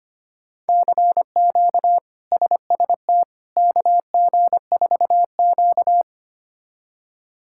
Morse code, heard as a keyed tone, about 25 wpm, 700 Hz. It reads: CQ SST KG4Q
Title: K1USN SST QSO Practice - 25wpm